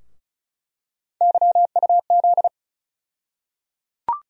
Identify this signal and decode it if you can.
Morse code, heard as a keyed tone, about 36 wpm, 700 Hz. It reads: YUZ
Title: USA Call Sign Postfixes - 3 Letters 35wpm